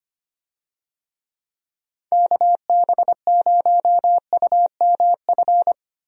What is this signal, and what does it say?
Morse code, heard as a keyed tone, about 25 wpm, 700 Hz. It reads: KB0UMF
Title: USA Call Signs - 6 Characters - 25wpm